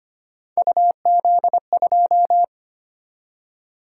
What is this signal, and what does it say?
Morse code, heard as a keyed tone, about 25 wpm, 700 Hz. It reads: UZ2
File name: Three Letter-Number Combinations - 25wpm